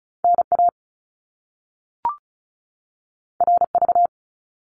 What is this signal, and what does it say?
Morse code, heard as a keyed tone, about 36 wpm, 700 Hz. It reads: NA RV
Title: Top Two Letter Combinations in Words 35wpm